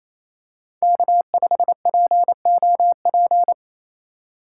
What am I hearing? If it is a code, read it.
Morse code, heard as a keyed tone, about 28 wpm, 700 Hz. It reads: K5POP
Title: USA Call Signs - 5 Characters - 2x Character Spacing - 28wpm